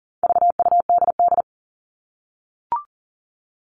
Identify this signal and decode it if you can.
Morse code, heard as a keyed tone, about 40 wpm, 700 Hz. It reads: VUDD
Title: All Cut Numbers - 4-Digit 40wpm